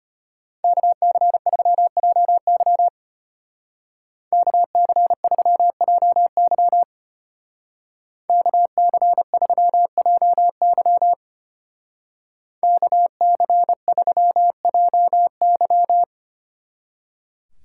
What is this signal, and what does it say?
Morse code, heard as a keyed tone, about 32 wpm, 700 Hz. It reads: KC3JY KC3JY KC3JY KC3JY